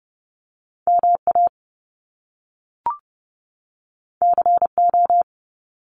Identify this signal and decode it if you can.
Morse code, heard as a keyed tone, about 30 wpm, 700 Hz. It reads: MA CO
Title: US States 30wpm